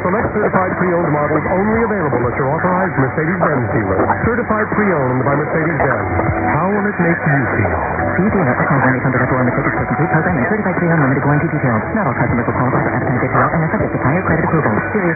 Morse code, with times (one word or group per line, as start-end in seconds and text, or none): none